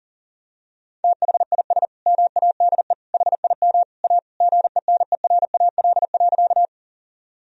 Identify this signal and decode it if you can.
Morse code, heard as a keyed tone, about 40 wpm, 700 Hz. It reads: THIS MADE HIM A GENERAL.
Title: Sentences from Top 1000 Words - Review Part 3 - 40wpm